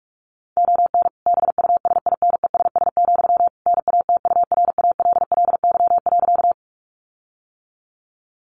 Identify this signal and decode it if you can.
Morse code, heard as a keyed tone, about 44 wpm, 700 Hz. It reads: ON BUSINESS, NATURALLY.
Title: Sentences from Top 1300 Words 45wpm